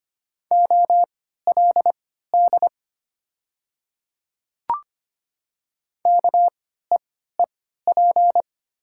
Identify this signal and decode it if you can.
Morse code, heard as a keyed tone, about 25 wpm, spaced as for 14 wpm, 700 Hz. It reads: OLD KEEP
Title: 3x Character Spacing - Top 500 Words - 25wpm